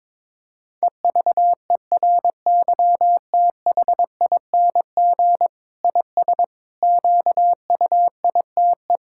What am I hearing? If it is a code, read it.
Morse code, heard as a keyed tone, about 22 wpm, 700 Hz. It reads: EVERYTHING IS QUITE
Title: Sentences from Top 1000 Words - Review Part 6 - 22wpm